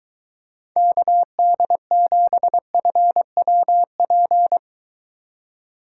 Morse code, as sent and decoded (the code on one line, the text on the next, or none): -.- -.. --... ..-. .-- .--.
KD7FWP